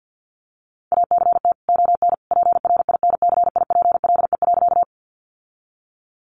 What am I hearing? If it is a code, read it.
Morse code, heard as a keyed tone, about 50 wpm, 700 Hz. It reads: ACT ON PRINCIPLE.
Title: Sentences from Top 1400 Words 50wpm